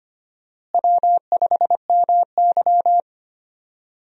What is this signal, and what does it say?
Morse code, heard as a keyed tone, about 25 wpm, 700 Hz. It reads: W5MY
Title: USA Call Signs - 4 Characters - Repeated 3x - 25wpm